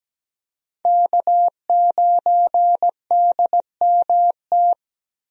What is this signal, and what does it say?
Morse code, heard as a keyed tone, about 17 wpm, 700 Hz. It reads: K9DMT
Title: USA Call Signs - 5 Characters - 17wpm